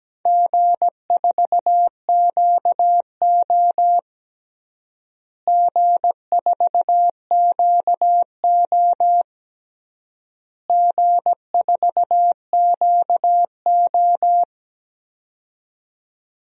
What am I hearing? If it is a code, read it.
Morse code, heard as a keyed tone, about 17 wpm, 700 Hz. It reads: G4QO G4QO G4QO
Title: DXCC-100 Call Signs - 4 Characters - Repeated 3x - 17wpm